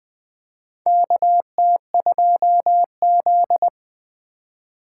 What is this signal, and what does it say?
Morse code, heard as a keyed tone, about 20 wpm, 700 Hz. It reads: KT2Z
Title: USA Call Signs - 4 Characters - 4x Character Spacing - 20wpm